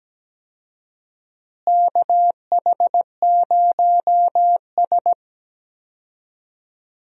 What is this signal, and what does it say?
Morse code, heard as a keyed tone, about 17 wpm, 700 Hz. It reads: KH0S
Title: USA Call Signs - 4 Characters - Repeated 3x - 17wpm